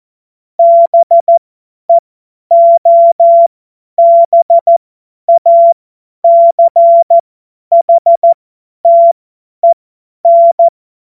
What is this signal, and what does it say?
Morse code, heard as a keyed tone, about 14 wpm, spaced as for 10 wpm, 680 Hz. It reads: BEOBACHTEN
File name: Morsekurs / Witze / Witz 27. Juli 2021, 50 F70